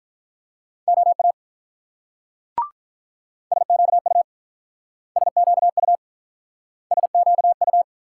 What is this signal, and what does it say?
Morse code, heard as a keyed tone, about 44 wpm, 690 Hz. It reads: OA SQU SQU SQU